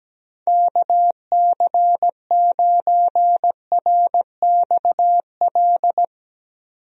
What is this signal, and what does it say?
Morse code, heard as a keyed tone, about 17 wpm, 700 Hz. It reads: KC9RXL
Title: USA Call Signs - 6 Characters - 17wpm